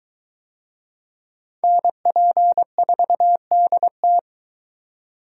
Morse code, heard as a keyed tone, about 21 wpm, 700 Hz. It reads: NP4DT